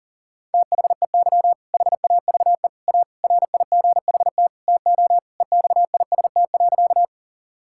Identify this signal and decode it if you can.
Morse code, heard as a keyed tone, about 40 wpm, 690 Hz. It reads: THEY HAVE A RIGHT TO EXIST.
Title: Sentences from Top 1000 Words - Review Part 5 - 40wpm